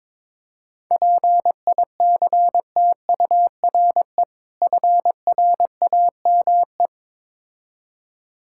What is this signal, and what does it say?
Morse code, heard as a keyed tone, about 22 wpm, 700 Hz. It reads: PICTURE FRAME
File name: Difficult Words with Hints - 22wpm